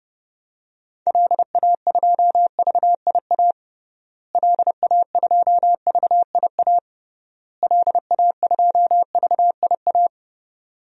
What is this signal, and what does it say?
Morse code, heard as a keyed tone, about 31 wpm, 700 Hz. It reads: LA2VIA LA2VIA LA2VIA
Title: DXCC-100 Call Signs - 6 Characters - Repeated 3x - 30wpm